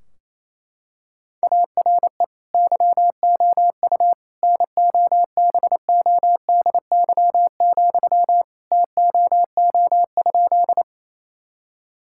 Morse code, heard as a keyed tone, about 28 wpm, 700 Hz. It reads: ARE YOU NOBODY, TOO?
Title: Sentences from Top 1600 Words 28wpm